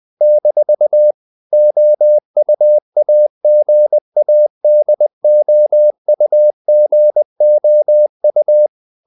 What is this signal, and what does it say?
Morse code, heard as a keyed tone, about 20 wpm, 600 Hz. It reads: - OUAGADOUGOU
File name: CW / Practice / NationalCapitals20